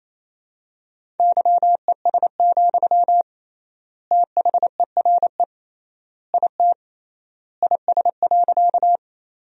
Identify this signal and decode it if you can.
Morse code, heard as a keyed tone, about 28 wpm, 700 Hz. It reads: YES, THERE IT IS.
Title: 3x Word-Spacing - Sentences from Top 500 Words - 28wpm